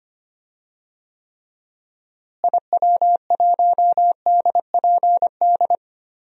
Morse code, heard as a keyed tone, about 25 wpm, 700 Hz. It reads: IW1DPD
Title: DXCC-100 Call Signs - 6 Characters - Repeated 2x - 25wpm